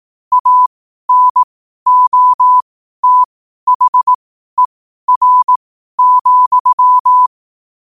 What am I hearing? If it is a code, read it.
Morse code, heard as a keyed tone, about 18 wpm, spaced as for 12 wpm, 1000 Hz. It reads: ANOTHER,
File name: qotdfiles / 12wpm